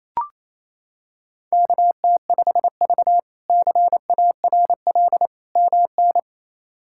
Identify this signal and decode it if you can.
Morse code, heard as a keyed tone, about 28 wpm, 700 Hz. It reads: KT5V CARL MN
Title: K1USN SST QSO Practice - 28wpm